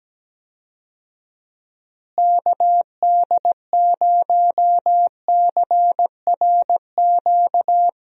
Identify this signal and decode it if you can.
Morse code, heard as a keyed tone, about 17 wpm, 700 Hz. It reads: KD0CRQ
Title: USA Call Signs - 6 Characters - Repeated 3x - 17wpm